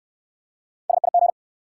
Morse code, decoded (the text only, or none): SIG